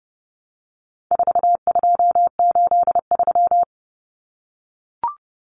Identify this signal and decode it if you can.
Morse code, heard as a keyed tone, about 30 wpm, 700 Hz. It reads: 4283
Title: Common Cut Numbers - 4-Digit 30wpm